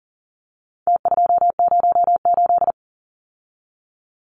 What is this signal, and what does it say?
Morse code, heard as a keyed tone, about 40 wpm, 700 Hz. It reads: T208
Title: Numbers - Contest Style - 4-Digit 40wpm